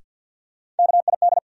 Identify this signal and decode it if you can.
Morse code, heard as a keyed tone, about 53 wpm, 700 Hz. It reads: KID
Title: Sets of 5 Words - New Word Drill - 50wpm